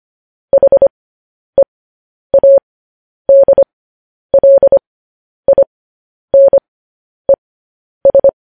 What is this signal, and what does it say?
Morse code, heard as a keyed tone, about 25 wpm, spaced as for 10 wpm, 550 Hz. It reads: HEADLINES
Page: news headlines in morse code at 10 wpm